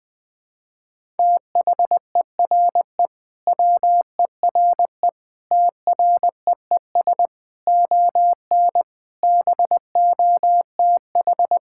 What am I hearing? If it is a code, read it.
Morse code, heard as a keyed tone, about 21 wpm, 700 Hz. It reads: THERE WERE TREES ON BOTH
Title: Sentences with 5-Word Phrases - 20wpm